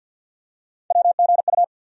Morse code, heard as a keyed tone, about 53 wpm, 700 Hz. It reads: WGU
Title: USA Call Sign Postfixes - 3 Letters 50wpm